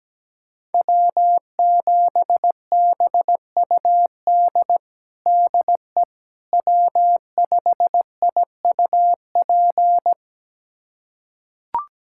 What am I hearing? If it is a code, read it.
Morse code, heard as a keyed tone, about 17 wpm, 700 Hz. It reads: W7BUD DE W5IUP
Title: USA Callsign DE Callsign - 5 Characters - 17wpm